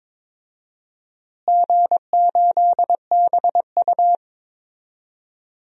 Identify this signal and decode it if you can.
Morse code, heard as a keyed tone, about 22 wpm, 700 Hz. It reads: G8BU
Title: DXCC-100 Call Signs - 4 Characters - Repeated 3x - 22wpm